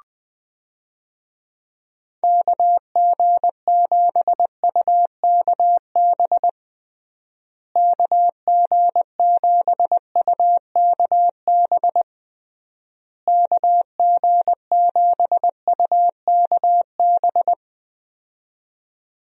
Morse code, heard as a keyed tone, about 20 wpm, 700 Hz. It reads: KG7UKB KG7UKB KG7UKB
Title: USA Call Signs - 6 Characters - Repeated 3x - 20wpm